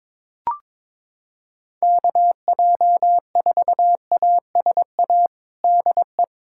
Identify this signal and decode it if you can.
Morse code, heard as a keyed tone, about 22 wpm, 700 Hz. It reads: KJ4ASA DE
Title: USA Callsign DE Callsign - 6 Characters - 22wpm